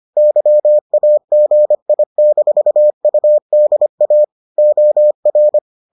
Morse code, heard as a keyed tone, about 25 wpm, 600 Hz. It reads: YAGI-UDA OR